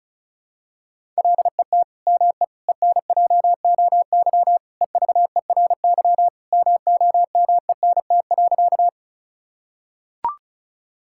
Morse code, heard as a keyed tone, about 34 wpm, 700 Hz. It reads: LET ME ENJOY EVERY MOMENT.
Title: Sentences from Top 1000 Words - Review Part 5 - 35wpm